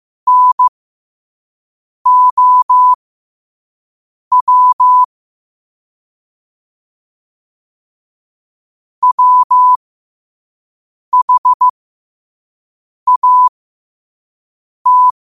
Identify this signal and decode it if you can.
Morse code, heard as a keyed tone, about 15 wpm, spaced as for 5 wpm, 1000 Hz. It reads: NOW WHAT
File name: qotdfiles / 5wpm